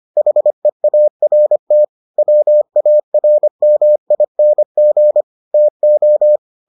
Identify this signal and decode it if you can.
Morse code, heard as a keyed tone, about 25 wpm, 600 Hz. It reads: HEART WARMING TO